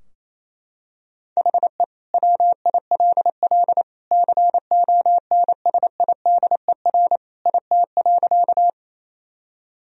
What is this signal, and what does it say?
Morse code, heard as a keyed tone, about 27 wpm, 700 Hz. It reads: HE WILL CONSIDER IT.
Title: Sentences from Top 1000 Words - Review Part 4 - 28wpm